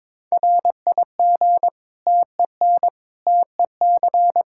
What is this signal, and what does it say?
Morse code, heard as a keyed tone, about 21 wpm, 700 Hz. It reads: RIG TEN TEC
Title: Practice QSO - Exchange 2 - Info - 22wpm